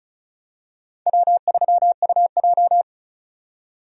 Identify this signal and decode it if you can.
Morse code, heard as a keyed tone, about 36 wpm, 700 Hz. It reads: W3UJ
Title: USA Call Signs - 4 Characters - 35wpm